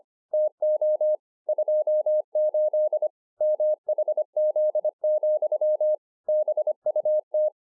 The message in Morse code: TO 28 MHZ, BUT

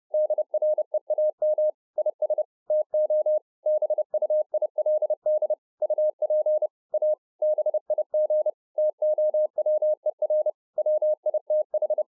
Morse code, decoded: DREAM IS TO BUILD UP A BIG TOWER WITH

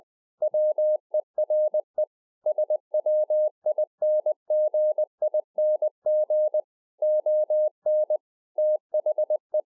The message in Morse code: WERE SWINGING ON THE

